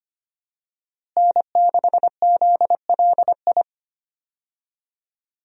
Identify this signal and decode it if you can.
Morse code, heard as a keyed tone, about 25 wpm, 700 Hz. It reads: N6ZLI
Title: USA Call Signs - 5 Characters - Repeated 2x - 25wpm